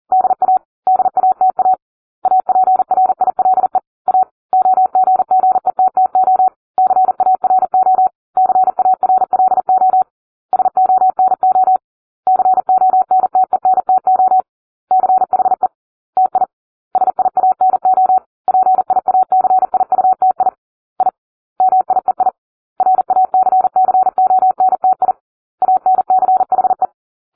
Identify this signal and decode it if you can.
Morse code, heard as a keyed tone, about 40 wpm, 750 Hz. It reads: DA DATA APRILE A OGGETTO CARO CARLO SONO CONTENTO CHE TI SIANO PIACIUTI I MIEI RACCONTI ANCHE